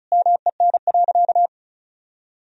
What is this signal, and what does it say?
Morse code, heard as a keyed tone, about 34 wpm, 700 Hz. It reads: MEN.
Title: Sentences from Top 1000 Words - Review Part 1 - 35wpm